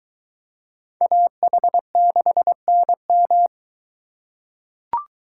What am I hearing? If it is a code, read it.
Morse code, heard as a keyed tone, about 22 wpm, 700 Hz. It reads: AH6NM